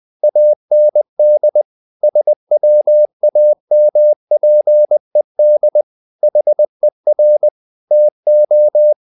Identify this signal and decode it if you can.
Morse code, heard as a keyed tone, about 20 wpm, 600 Hz. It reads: AND SWAMPED HER TO
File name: the old man and the sea part 2 20wpm